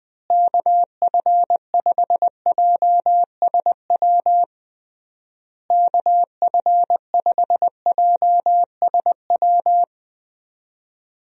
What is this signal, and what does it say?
Morse code, heard as a keyed tone, about 20 wpm, 700 Hz. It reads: KF5JSW KF5JSW
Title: USA Call Signs - 6 Characters - Repeated 3x - 20wpm